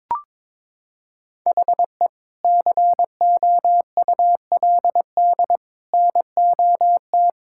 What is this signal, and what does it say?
Morse code, heard as a keyed tone, about 22 wpm, 700 Hz. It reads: HE COULD NOT
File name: practice / 22wpm / Sentences with 3-Word Phrases - 22wpm